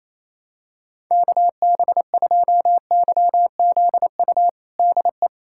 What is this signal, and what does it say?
Morse code, heard as a keyed tone, about 28 wpm, 700 Hz. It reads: KB2YZU DE
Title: USA Callsign DE Callsign 28wpm